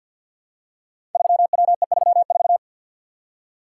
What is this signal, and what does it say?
Morse code, heard as a keyed tone, about 52 wpm, 690 Hz. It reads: 3WE34
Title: Code Groups - 5 Letter:Numbers - 50wpm